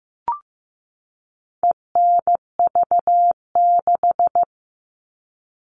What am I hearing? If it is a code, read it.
Morse code, heard as a keyed tone, about 15 wpm, 700 Hz. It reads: ENV6
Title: All Cut Numbers - 4-Digit 15wpm